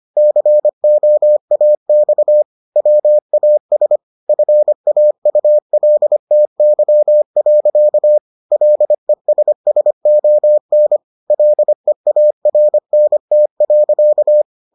Morse code, heard as a keyed tone, about 25 wpm, 600 Hz. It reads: COAX WAS FAULTY. LESSON LEARNT.